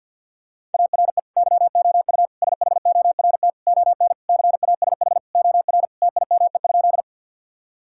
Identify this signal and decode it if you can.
Morse code, heard as a keyed tone, about 50 wpm, 690 Hz. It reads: ARE YOU SHORT ON CASH OR TIME?
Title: Sentences from Top 1000 Words - Review Part 5 - 50wpm